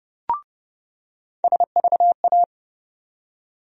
Morse code, heard as a keyed tone, about 30 wpm, 700 Hz. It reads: SVA
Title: USA Call Sign Postfixes - 3 Letters 30wpm